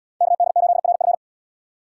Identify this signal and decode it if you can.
Morse code, heard as a keyed tone, about 77 wpm, 700 Hz. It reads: KD8RF